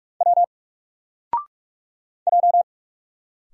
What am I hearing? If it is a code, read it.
Morse code, heard as a keyed tone, about 44 wpm, 700 Hz. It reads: W J
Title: Single Character (Letter:Number:Punctuation:Prosign) Alt 45wpm